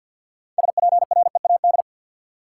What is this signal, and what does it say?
Morse code, heard as a keyed tone, about 51 wpm, 700 Hz. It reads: SPREAD
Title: Six Letter Words - 50wpm